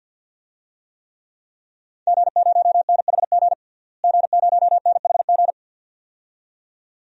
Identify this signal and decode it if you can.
Morse code, heard as a keyed tone, about 47 wpm, 700 Hz. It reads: G0NHG G0NHG
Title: DXCC-100 Call Signs - 5 Characters - Repeated 2x - 50wpm